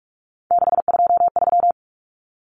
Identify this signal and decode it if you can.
Morse code, heard as a keyed tone, about 44 wpm, 700 Hz. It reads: 623
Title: Common Cut Numbers - 3-Digit 45wpm